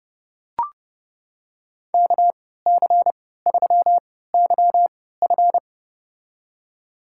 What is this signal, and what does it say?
Morse code, heard as a keyed tone, about 32 wpm, spaced as for 17 wpm, 700 Hz. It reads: KC3YF
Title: USA Call Signs - 3x Character Spacing - 30wpm